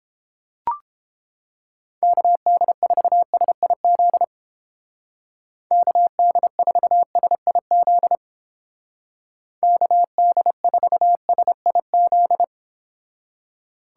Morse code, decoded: KD4SIZ KD4SIZ KD4SIZ

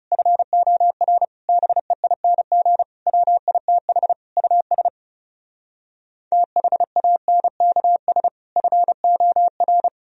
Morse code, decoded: FOR BEING WITH US THANKS FOR

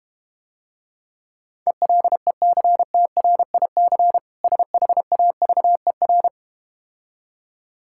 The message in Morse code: ELECTRIC SHAVER

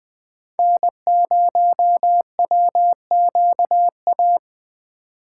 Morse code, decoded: N0WQA

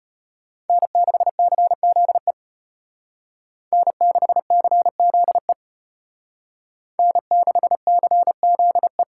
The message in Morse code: N6CZE N6CZE N6CZE

